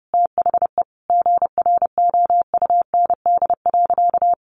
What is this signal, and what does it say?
Morse code, heard as a keyed tone, about 30 wpm, 700 Hz. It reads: THE GROUND.
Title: Sentences from Top 1500 Words 30wpm